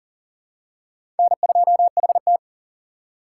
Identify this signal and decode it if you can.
Morse code, heard as a keyed tone, about 40 wpm, 700 Hz. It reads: N2HT
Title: USA Call Signs - 4 Characters - 3x Character Spacing - 40wpm